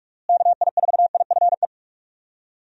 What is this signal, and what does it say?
Morse code, heard as a keyed tone, about 46 wpm, 700 Hz. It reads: KI4IFE
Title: USA Call Signs - 6 Characters - Repeated 3x - 45wpm